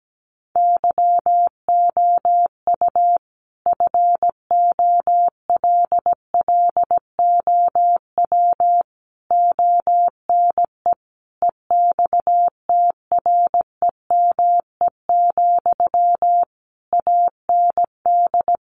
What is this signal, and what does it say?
Morse code, heard as a keyed tone, about 17 wpm, 700 Hz. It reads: YOU FOLLOW ONE EXTREME, AND
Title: Sentences from Top 1300 Words 17wpm